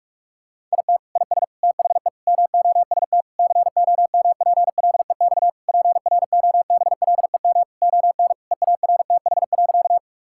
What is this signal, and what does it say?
Morse code, heard as a keyed tone, about 46 wpm, 700 Hz. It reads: IT IS THE MOST COMPLEX PROBLEM ON EARTH.
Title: Sentences from Top 1000 Words - Review Part 6 - 45wpm